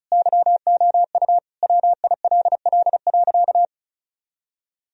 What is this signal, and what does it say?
Morse code, heard as a keyed tone, about 34 wpm, 700 Hz. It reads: YOU WILL.
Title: Sentences from Top 1000 Words - Review Part 2 - 35wpm